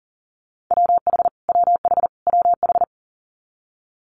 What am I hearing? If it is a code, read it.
Morse code, heard as a keyed tone, about 43 wpm, 700 Hz. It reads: WH WH WH
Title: Multi-letter Phonemes Repeated 3x 40wpm